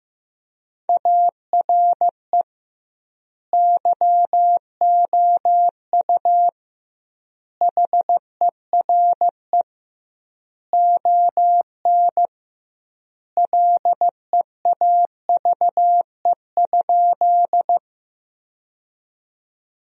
Morse code, heard as a keyed tone, about 15 wpm, 700 Hz. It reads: ARE YOU HERE ON LEAVE?
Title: 2x Word-Spacing - Sentences from Top 400 Words - 15wpm